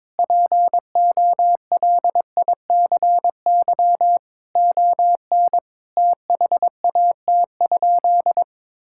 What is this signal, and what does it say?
Morse code, heard as a keyed tone, about 22 wpm, 700 Hz. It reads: POLICY ON THAT?
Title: Sentences from Top 1000 Words - Review Part 5 - 22wpm